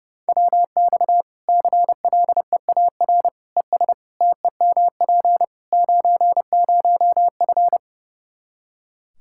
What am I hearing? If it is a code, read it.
Morse code, heard as a keyed tone, about 32 wpm, 700 Hz. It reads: WX CLEAR ES TEMP 90F
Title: Practice QSO - Exchange 2 - Info - 30wpm